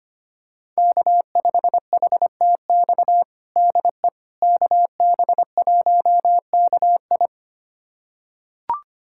Morse code, heard as a keyed tone, about 25 wpm, 700 Hz. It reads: K5HTX DE KB1KI
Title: USA Callsign DE Callsign - 5 Characters - 25wpm